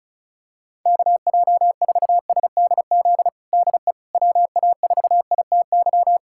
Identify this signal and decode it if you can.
Morse code, heard as a keyed tone, about 36 wpm, 700 Hz. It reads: KJ4SDZ DE WA4ITY
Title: USA Callsign DE Callsign - 6 Characters - 35wpm